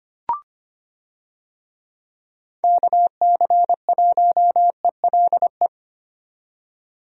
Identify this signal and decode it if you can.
Morse code, heard as a keyed tone, about 25 wpm, 700 Hz. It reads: KC1ELE
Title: USA Call Signs - 6 Characters - 25wpm